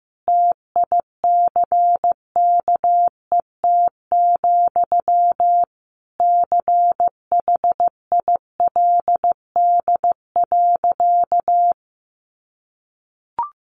Morse code, heard as a keyed tone, about 15 wpm, 700 Hz. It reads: TICKET, CHILD.
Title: Sentences from Top 1400 Words 15wpm